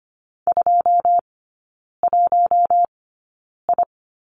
Morse code, heard as a keyed tone, about 25 wpm, spaced as for 9 wpm, 700 Hz. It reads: 21I
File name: Lesson 15e - Number 5 - Three Characters 25wpm